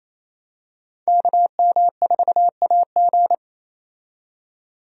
Morse code, heard as a keyed tone, about 28 wpm, 700 Hz. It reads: KM4AG
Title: USA Call Signs - 5 Characters - 28wpm